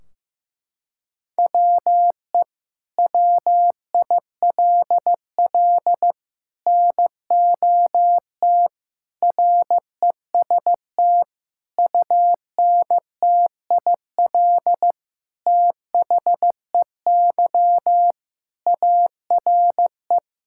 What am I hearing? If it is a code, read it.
Morse code, heard as a keyed tone, about 16 wpm, 700 Hz. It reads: WE WILL NOT REST UNTIL THEY ARE
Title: Sentences from Top 1600 Words 15wpm